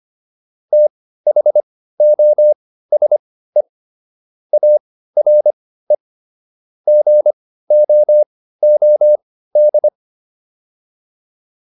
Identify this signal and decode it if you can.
Morse code, heard as a keyed tone, about 25 wpm, spaced as for 15 wpm, 600 Hz. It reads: THOSE ARE GOOD